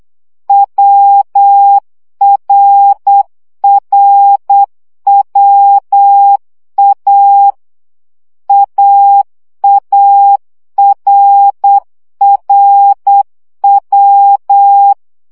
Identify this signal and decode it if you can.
Morse code, heard as a keyed tone, about 8 wpm, 800 Hz. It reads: WRRWA AARRW
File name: CW gyakorlas a r w 7 baud